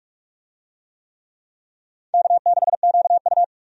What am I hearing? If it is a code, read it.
Morse code, heard as a keyed tone, about 46 wpm, 700 Hz. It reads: K6QU
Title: USA Call Signs - 4 Characters - 45wpm